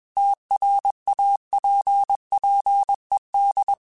Morse code, heard as a keyed tone, about 21 wpm, 800 Hz. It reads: TRAPPED
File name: QSOs / QSOs-mp3s / kc7sac@20